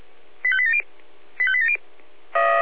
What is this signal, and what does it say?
Five-tone selective call (ZVEI), heard as quick runs of five tones, tones of about 70 ms. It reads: 85870 85870